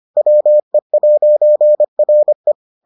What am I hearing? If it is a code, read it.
Morse code, heard as a keyed tone, about 25 wpm, 600 Hz. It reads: WE'RE